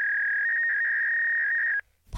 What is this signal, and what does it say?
Fax machine handshake.
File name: prev